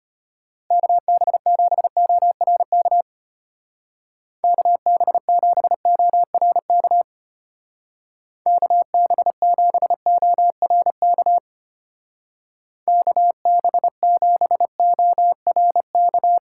Morse code, heard as a keyed tone, about 31 wpm, 700 Hz. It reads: KB7ORK KB7ORK KB7ORK KB7ORK
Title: Speed Racing - USA 2x3 Call Signs 25wpm